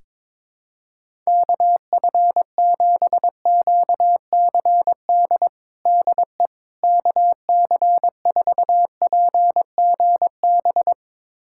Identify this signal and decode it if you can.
Morse code, heard as a keyed tone, about 22 wpm, 700 Hz. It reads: KF7QCD DE KC4PGB
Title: USA Callsign DE Callsign 22wpm